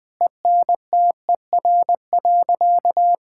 Morse code, heard as a keyed tone, about 20 wpm, 700 Hz. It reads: ENTER.
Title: Sentences with 6-Word Phrases - 20wpm